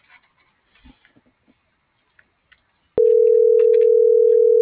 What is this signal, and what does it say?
Ringback tone (North America), 440 + 480 Hz.